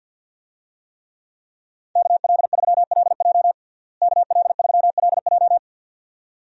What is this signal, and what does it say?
Morse code, heard as a keyed tone, about 48 wpm, 700 Hz. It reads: KL3LJ KL3LJ
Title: USA Call Signs - 5 Characters - Repeated 2x - 50wpm